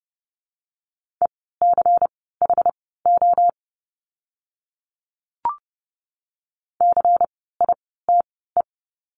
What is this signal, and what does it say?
Morse code, heard as a keyed tone, about 30 wpm, spaced as for 17 wpm, 700 Hz. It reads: ECHO CITE
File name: Lesson 16f - Letter c - Words 30wpm